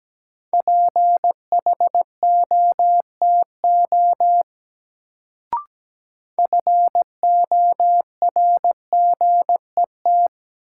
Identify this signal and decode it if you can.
Morse code, heard as a keyed tone, about 17 wpm, 700 Hz. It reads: PHOTO FORGET
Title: Sentences from Top 1200 Words - New Word Drill - 17wpm